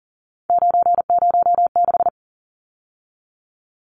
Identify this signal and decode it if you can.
Morse code, heard as a keyed tone, about 40 wpm, 700 Hz. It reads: 906
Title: Numbers - 3-Digit 40wpm